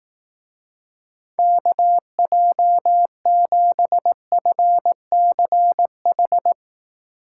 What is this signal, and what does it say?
Morse code, heard as a keyed tone, about 18 wpm, 700 Hz. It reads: KJ7FCH
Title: Speed Racing - USA 2x3 Call Signs 15wpm